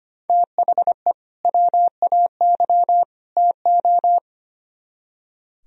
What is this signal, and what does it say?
Morse code, heard as a keyed tone, about 32 wpm, 700 Hz. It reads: THE WAY TO